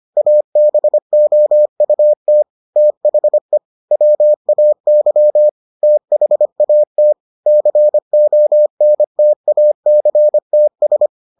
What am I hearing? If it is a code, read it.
Morse code, heard as a keyed tone, about 25 wpm, 600 Hz. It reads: ABOUT THE WAY THAT CONTACTS